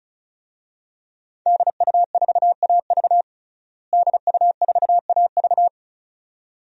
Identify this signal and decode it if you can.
Morse code, heard as a keyed tone, about 36 wpm, 700 Hz. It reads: DU4AV DU4AV
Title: DXCC-100 Call Signs - 5 Characters - Repeated 2x - 35wpm